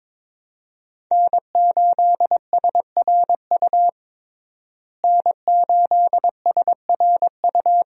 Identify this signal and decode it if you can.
Morse code, heard as a keyed tone, about 21 wpm, 700 Hz. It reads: N8SRU N8SRU
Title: USA Call Signs - 5 Characters - Repeated 2x - 22wpm